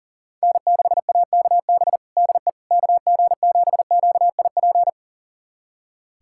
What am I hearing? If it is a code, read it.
Morse code, heard as a keyed tone, about 37 wpm, 700 Hz. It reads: N6AKB DE KG7QIP
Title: USA Callsign DE Callsign 40wpm